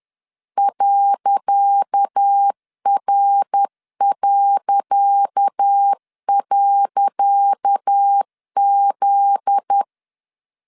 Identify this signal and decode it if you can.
Morse code, heard as a keyed tone, about 10 wpm, 800 Hz. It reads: .R..Z
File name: LEZIONI / 10 SIGNS / 10C DOT 10wpm 2940